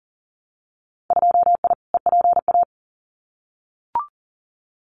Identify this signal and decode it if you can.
Morse code, heard as a keyed tone, about 40 wpm, 700 Hz. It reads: 2I EPA
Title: Winter Field Day Exchanges 40wpm